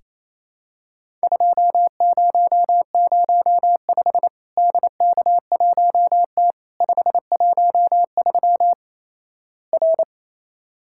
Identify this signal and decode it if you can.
Morse code, heard as a keyed tone, about 28 wpm, 700 Hz. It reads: 2005 DK1T 513 R
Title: Worked All Europe DX Contest - QTC Exchange - 28wpm